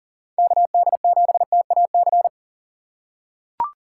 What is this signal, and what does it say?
Morse code, heard as a keyed tone, about 42 wpm, 690 Hz. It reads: KD7TAC
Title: USA Call Signs - 6 Characters - 2x Character Spacing - 40wpm